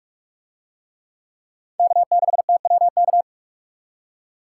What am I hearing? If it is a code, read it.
Morse code, heard as a keyed tone, about 44 wpm, 700 Hz. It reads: K6TWK